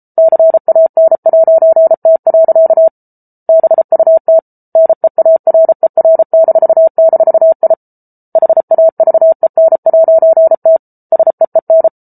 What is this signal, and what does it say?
Morse code, heard as a keyed tone, about 34 wpm, 650 Hz. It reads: CAN'T. BUT NEARER--I HAVEN'T SEEN